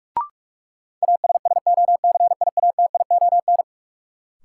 Morse code, heard as a keyed tone, about 44 wpm, 700 Hz. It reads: ASSOCIATION
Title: Sentences from Top 1100 Words - New Word Drill - 45wpm